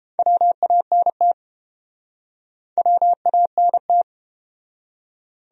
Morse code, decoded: WANT WANT

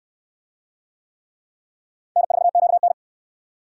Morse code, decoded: N4QN

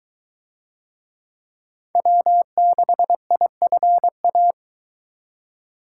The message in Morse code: W6IFA

